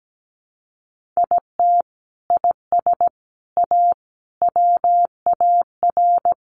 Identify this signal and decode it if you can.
Morse code, heard as a keyed tone, about 17 wpm, 700 Hz. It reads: IT IS A WAR
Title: Sentences from Top 1500 Words 17wpm